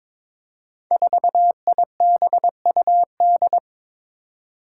Morse code, heard as a keyed tone, about 22 wpm, 700 Hz. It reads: 4IBUD
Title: Five Challenging Letter-Number Combinations 22wpm